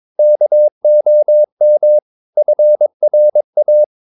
Morse code, heard as a keyed tone, about 22 wpm, 600 Hz. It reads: KOM FRA
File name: Ferie QRS 22wpm